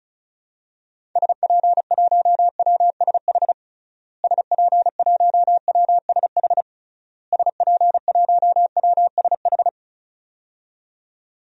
Morse code, heard as a keyed tone, about 36 wpm, 700 Hz. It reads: SP1WSH SP1WSH SP1WSH
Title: DXCC-100 Call Signs - 6 Characters - Repeated 3x - 35wpm